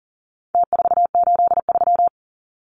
Morse code, heard as a keyed tone, about 40 wpm, 700 Hz. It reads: T483
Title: Numbers - Contest Style - 4-Digit 40wpm